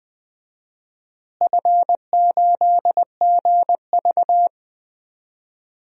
Morse code, heard as a keyed tone, about 20 wpm, 700 Hz. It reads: F8GV
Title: DXCC-100 Call Signs - 4 Characters - Repeated 2x - 20wpm